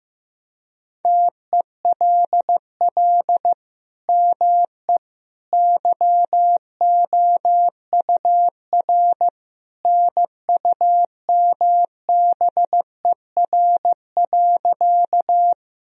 Morse code, heard as a keyed tone, about 15 wpm, 700 Hz. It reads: TELL ME YOUR NUMBER.
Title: Sentences from Top 1000 Words - Review Part 1 - 15wpm